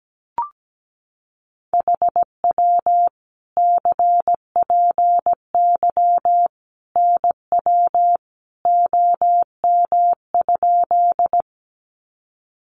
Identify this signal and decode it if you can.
Morse code, heard as a keyed tone, about 17 wpm, 700 Hz. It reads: HW CPY NW OM?
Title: Common QSO Phrases 17wpm